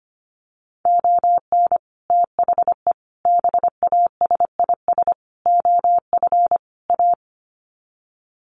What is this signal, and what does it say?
Morse code, heard as a keyed tone, about 25 wpm, 700 Hz. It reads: ON THE BASIS OF A
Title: Sets of 5 Words Encore 25wpm